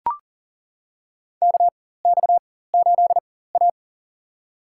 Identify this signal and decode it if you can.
Morse code, heard as a keyed tone, about 37 wpm, spaced as for 18 wpm, 700 Hz. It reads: KX8A
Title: USA Call Signs - 4 Characters - 4x Character Spacing - 40wpm